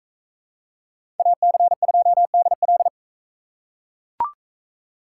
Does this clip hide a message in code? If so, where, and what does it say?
Yes: Morse code from 1.2 to 2.9 s, heard as a keyed tone, about 40 wpm, 700 Hz. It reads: AC2DL